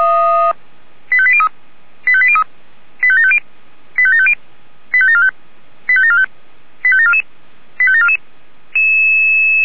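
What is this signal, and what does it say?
Five-tone selective call (ZVEI), heard as quick runs of five tones, tones of about 70 ms. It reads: 85803 85803 85750 85750 85745 85745 85744 85744
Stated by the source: ZVEI five-tone selective call received on an SDR.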